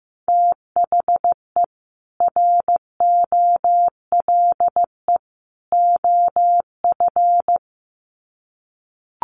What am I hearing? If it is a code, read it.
Morse code, heard as a keyed tone, about 15 wpm, 700 Hz. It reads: THE ROLE OF
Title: Sets of 3 Words 15wpm